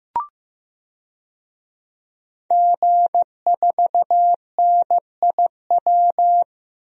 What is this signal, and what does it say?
Morse code, heard as a keyed tone, about 15 wpm, 700 Hz. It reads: G4NIW
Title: DXCC-100 Call Signs - 5 Characters - 15wpm